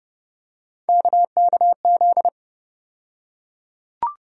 Morse code, heard as a keyed tone, about 30 wpm, 700 Hz. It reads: KKZ
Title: USA Call Sign Postfixes - 3 Letters 30wpm